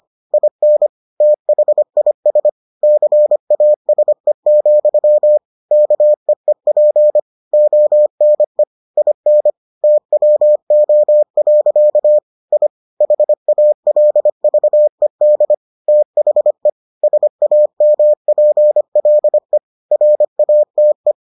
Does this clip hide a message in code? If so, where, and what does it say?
Yes: Morse code from 0.3 to 21.1 s, heard as a keyed tone, about 25 wpm, 600 Hz. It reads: IN THIS CASE, KEEP ONE IN TWO. I HALVED THE SAMPLE RATE